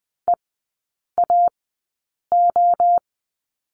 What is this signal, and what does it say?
Morse code, heard as a keyed tone, about 20 wpm, spaced as for 8 wpm, 700 Hz. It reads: EAO
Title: 041 Lesson 8e - Number 1 - Three Characters 20wpm